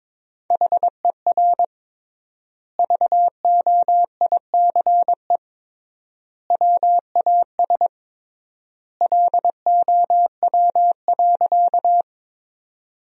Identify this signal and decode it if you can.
Morse code, heard as a keyed tone, about 22 wpm, 700 Hz. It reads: HER VOICE WAS LOW.
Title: 3x Word-Spacing - Sentences from Top 500 Words - 22wpm